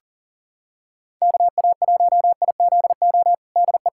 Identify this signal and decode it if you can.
Morse code, heard as a keyed tone, about 40 wpm, 700 Hz. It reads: KA1IZO DE
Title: USA Callsign DE Callsign - 6 Characters - 40wpm